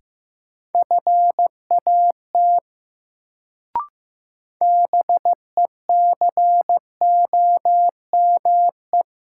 Morse code, heard as a keyed tone, about 15 wpm, 700 Hz. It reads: FAT BECOME
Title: Sentences from Top 600 Words - New Word Drill - 15wpm